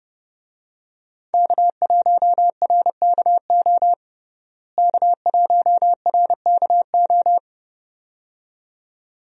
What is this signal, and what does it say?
Morse code, heard as a keyed tone, about 32 wpm, 700 Hz. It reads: K1RKO K1RKO
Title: USA Call Signs - 5 Characters - Repeated 2x - 30wpm